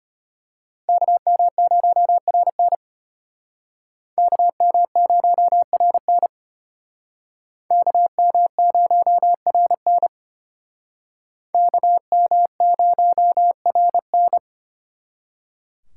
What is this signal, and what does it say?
Morse code, heard as a keyed tone, about 34 wpm, 700 Hz. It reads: KM0RN KM0RN KM0RN KM0RN